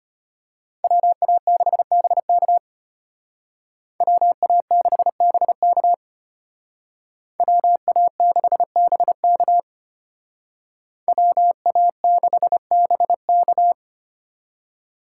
Morse code, heard as a keyed tone, about 32 wpm, 700 Hz. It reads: WA6BK WA6BK WA6BK WA6BK